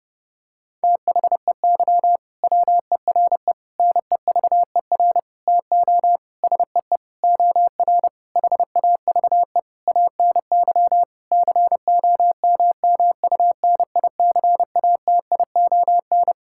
Morse code, heard as a keyed tone, about 30 wpm, 700 Hz. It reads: THEY WERE NEVER TO SEE OR HAVE ANY COMMUNICATION
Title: Sentences from Top 1000 Words - Review Part 6 - 30wpm